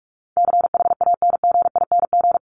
Morse code, heard as a keyed tone, about 44 wpm, 700 Hz. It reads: CHANGING
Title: ING Suffix - Top 100 Words with Base Words 45wpm